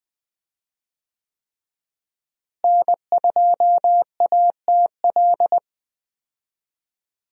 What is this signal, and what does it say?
Morse code, heard as a keyed tone, about 20 wpm, 700 Hz. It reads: N2ATL
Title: USA Call Signs - 5 Characters - Repeated 3x - 20wpm